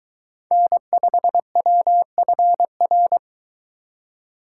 Morse code, heard as a keyed tone, about 23 wpm, 700 Hz. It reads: N5WFR